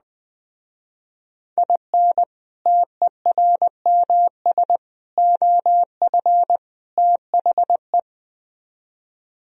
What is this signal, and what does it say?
Morse code, heard as a keyed tone, about 20 wpm, 700 Hz. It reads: IN TERMS OF THE
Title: Speed-Racing - Sets of 4 Words 17wpm